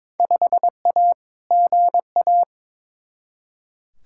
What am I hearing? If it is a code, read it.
Morse code, heard as a keyed tone, about 22 wpm, 700 Hz. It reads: 5A GA
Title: Field Day Exchanges - 22wpm